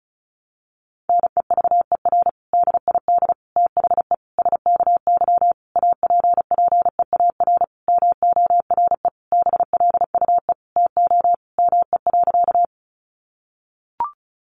Morse code, heard as a keyed tone, about 34 wpm, 700 Hz. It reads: NEVER DID THE SKY APPEAR MORE BLUE TO ME.
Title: Sentences from Top 1200 Words 35wpm